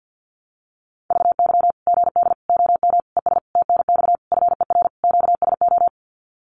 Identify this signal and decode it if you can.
Morse code, heard as a keyed tone, about 50 wpm, 700 Hz. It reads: VY GD OM ES TNX FER QSO
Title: Common QSO Phrases 50wpm